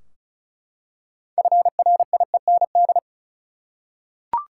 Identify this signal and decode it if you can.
Morse code, heard as a keyed tone, about 34 wpm, 700 Hz. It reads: FRIEND
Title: Six Letter Words - 35wpm